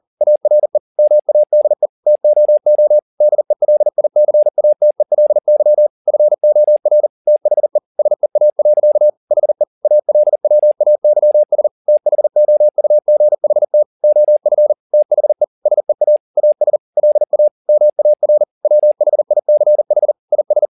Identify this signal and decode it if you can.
Morse code, heard as a keyed tone, about 40 wpm, 600 Hz. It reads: ARE MADE TOO DELICATELY FOR THE SEA. HE ALWAYS THOUGHT OF THE SEA AS LA MAR WHICH IS